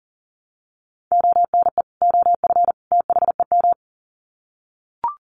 Morse code, heard as a keyed tone, about 40 wpm, 700 Hz. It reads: ONE OF THEM
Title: Sets of 3 Words 40wpm